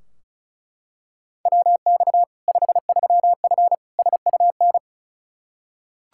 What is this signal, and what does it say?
Morse code, heard as a keyed tone, about 36 wpm, 700 Hz. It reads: WX 53F SUN
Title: Practice QSO - Exchange 2 - Info - 35wpm